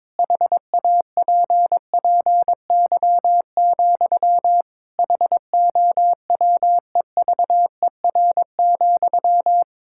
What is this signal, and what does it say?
Morse code, heard as a keyed tone, about 21 wpm, 700 Hz. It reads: HAPPY, HOWEVER,
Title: Sentences from Top 1000 Words - Review Part 6 - 22wpm